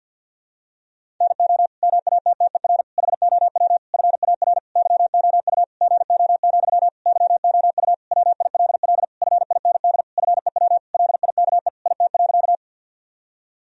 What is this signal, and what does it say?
Morse code, heard as a keyed tone, about 52 wpm, 700 Hz. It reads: NO MATTER HOW FAR YOU GO, YOU WILL FIND FEW LIKE IT.
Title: Sentences from Top 1000 Words - Review Part 1 - 50wpm